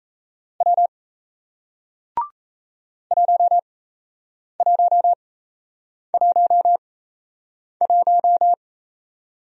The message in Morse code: W 1111

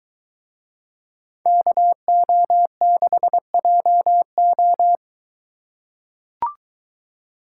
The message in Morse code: KO6JO

